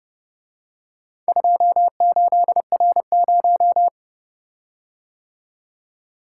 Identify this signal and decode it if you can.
Morse code, heard as a keyed tone, about 30 wpm, 700 Hz. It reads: 28R0